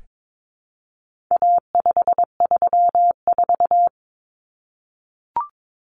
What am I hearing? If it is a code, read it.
Morse code, heard as a keyed tone, about 22 wpm, 700 Hz. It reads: A534
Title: Common Cut Numbers - 4-Digit 22wpm